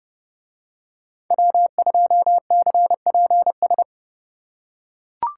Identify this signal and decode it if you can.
Morse code, heard as a keyed tone, about 29 wpm, 700 Hz. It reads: W2CPS